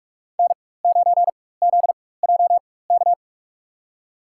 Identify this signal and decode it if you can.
Morse code, heard as a keyed tone, about 41 wpm, spaced as for 20 wpm, 700 Hz. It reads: N9ZJK
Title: USA Call Signs - 4x Character Spacing - 45wpm